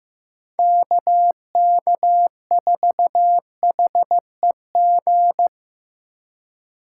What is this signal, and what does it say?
Morse code, heard as a keyed tone, about 15 wpm, 700 Hz. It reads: KK4HEG
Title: K1USN SST QSO Practice - 15wpm